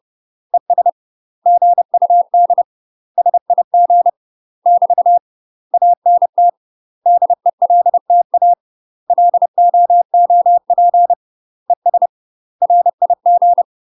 Morse code, heard as a keyed tone, about 30 wpm, 700 Hz. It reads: ES GUD SIG <BT> ANT DELTA LOOP ES RIG
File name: UK MASTER QSOs TRAINING / 30WPM / UK MASTER QSOs 0019